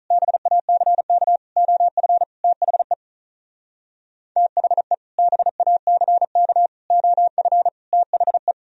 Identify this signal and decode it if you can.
Morse code, heard as a keyed tone, about 46 wpm, 700 Hz. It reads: BACK OF THE THE BACK OF THE